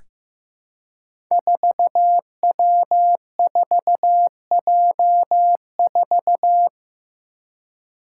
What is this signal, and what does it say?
Morse code, heard as a keyed tone, about 15 wpm, 700 Hz. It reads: 4W4J4
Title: Five Challenging Letter-Number Combinations 15wpm